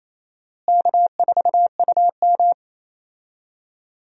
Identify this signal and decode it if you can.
Morse code, heard as a keyed tone, about 28 wpm, 700 Hz. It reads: K4UM
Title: USA Call Signs - 4 Characters - Repeated 3x - 28wpm